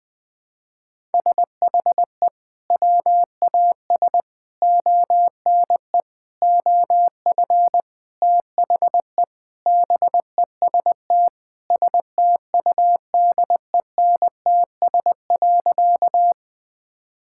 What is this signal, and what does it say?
Morse code, heard as a keyed tone, about 20 wpm, 700 Hz. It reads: SHE WAS ONE OF THE BEST STUDENTS.
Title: Sentences with 4-Word Phrases - Spoken Beginning - 20wpm